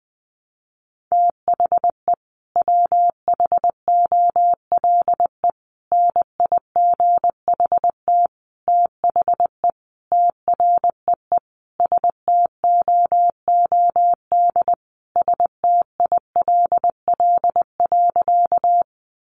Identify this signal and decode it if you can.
Morse code, heard as a keyed tone, about 20 wpm, 700 Hz. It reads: THE WHOLE NIGHT THE TREE STOOD STILL.
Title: Sentences from Top 500 Words 20wpm